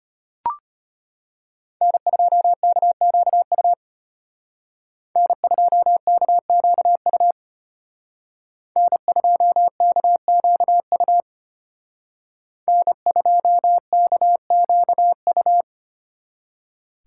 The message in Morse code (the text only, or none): N2KQU N2KQU N2KQU N2KQU